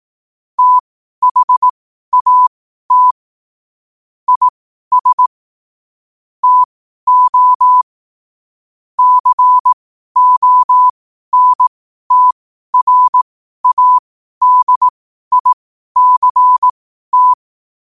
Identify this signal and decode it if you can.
Morse code, heard as a keyed tone, about 18 wpm, spaced as for 12 wpm, 1000 Hz. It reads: THAT IS TO CONTRADICT
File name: qotdfiles / 12wpm